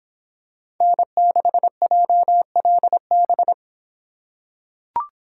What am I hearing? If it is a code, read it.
Morse code, heard as a keyed tone, about 26 wpm, 700 Hz. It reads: N6JLB